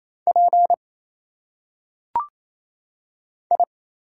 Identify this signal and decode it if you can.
Morse code, heard as a keyed tone, about 27 wpm, 700 Hz. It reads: P I
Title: Single Letters - Phonetic Pronunciation 28wpm